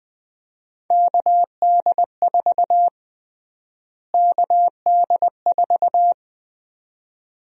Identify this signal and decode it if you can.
Morse code, heard as a keyed tone, about 20 wpm, 700 Hz. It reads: KD4 KD4
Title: USA Call Sign Prefixes - Repeated 2x - 20wpm